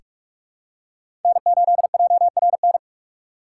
Morse code, heard as a keyed tone, about 44 wpm, 700 Hz. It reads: N8JRN
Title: USA Call Signs - 5 Characters - 6x Character Spacing - 45wpm